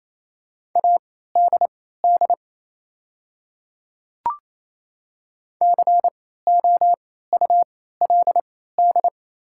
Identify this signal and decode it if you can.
Morse code, heard as a keyed tone, about 28 wpm, spaced as for 16 wpm, 700 Hz. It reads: ADD COULD
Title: 3x Character Spacing - Top 200 Words - 28wpm